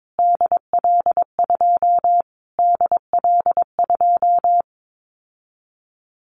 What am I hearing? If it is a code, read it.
Morse code, heard as a keyed tone, about 22 wpm, 700 Hz. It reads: DL2 DL2
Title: Callsign Prefixes for Top 25 DXCC - Repeated 3x - 22wpm